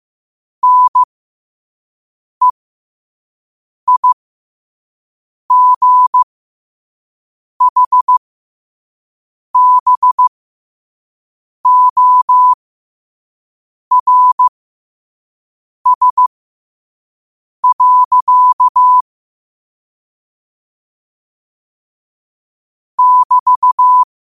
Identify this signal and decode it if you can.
Morse code, heard as a keyed tone, about 15 wpm, spaced as for 5 wpm, 1000 Hz. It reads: NEIGHBORS. <BT>
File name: qotdfiles / 5wpm